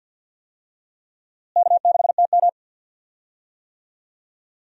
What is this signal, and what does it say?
Morse code, heard as a keyed tone, about 53 wpm, 700 Hz. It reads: K6TM
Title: USA Call Signs - 4 Characters - Repeated 2x - 50wpm